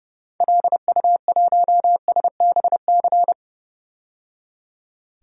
Morse code, heard as a keyed tone, about 30 wpm, 700 Hz. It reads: LU1SBC
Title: DXCC-100 Call Signs - 6 Characters - 30wpm